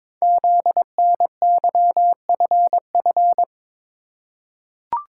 Morse code, heard as a keyed tone, about 22 wpm, 700 Hz. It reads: ZNYFF
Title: Code Groups - 5 Letters - 22wpm